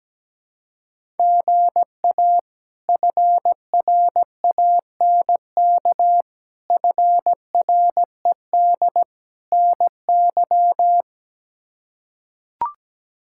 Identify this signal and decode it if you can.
Morse code, heard as a keyed tone, about 17 wpm, 700 Hz. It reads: GA FRANK FRED NY
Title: K1USN SST QSO Practice - 17wpm